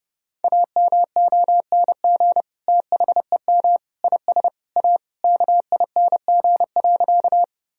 Morse code, heard as a keyed tone, about 28 wpm, 700 Hz. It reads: AMONG THEM IS A KING.
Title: Sentences from Top 1000 Words - Review Part 3 - 30wpm